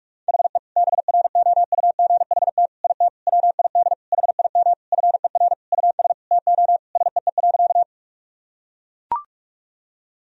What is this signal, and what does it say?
Morse code, heard as a keyed tone, about 42 wpm, 700 Hz. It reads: HE BROUGHT IT WID HIM FER US TO SEE.
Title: Sentences from Top 500 Words and Abbreviations 45wpm